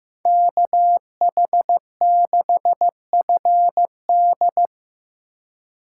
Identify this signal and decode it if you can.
Morse code, heard as a keyed tone, about 15 wpm, 700 Hz. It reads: KH6FD
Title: USA Call Signs - 5 Characters - 3x Character Spacing - 15wpm